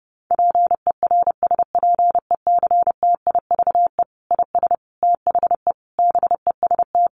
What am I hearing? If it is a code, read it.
Morse code, heard as a keyed tone, about 30 wpm, 700 Hz. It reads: PERSPECTIVE IS THE BEST
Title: Sentences from Top 1200 Words 30wpm